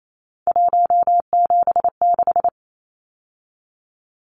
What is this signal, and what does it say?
Morse code, heard as a keyed tone, about 28 wpm, 700 Hz. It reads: 176
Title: Numbers - 3-Digit 28wpm